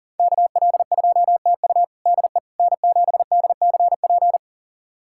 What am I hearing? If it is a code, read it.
Morse code, heard as a keyed tone, about 40 wpm, 700 Hz. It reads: KL2TU DE N7DCP
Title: USA Callsign DE Callsign - 5 Characters - 40wpm